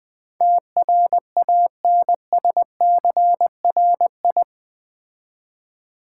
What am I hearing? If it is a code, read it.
Morse code, heard as a keyed tone, about 20 wpm, 700 Hz. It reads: TRANSCRI
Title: Build A Word - 4 Letters At A Time - 10-Letter Words - 20wpm